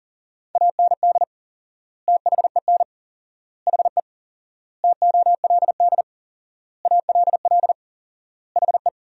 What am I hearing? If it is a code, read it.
Morse code, heard as a keyed tone, about 37 wpm, 700 Hz. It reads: AND THEN HE TOLD ALL HE
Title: 4x Word-Spacing - Sentences from Top 400 Words - 40wpm